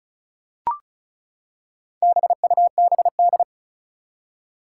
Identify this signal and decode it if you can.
Morse code, heard as a keyed tone, about 36 wpm, 700 Hz. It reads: BUBD
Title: Four Challenging Letter-Number Combinations 35wpm